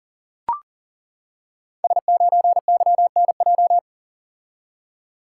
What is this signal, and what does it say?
Morse code, heard as a keyed tone, about 38 wpm, 690 Hz. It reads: S9YNJ
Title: Code Groups - 5 Letter:Numbers - 40wpm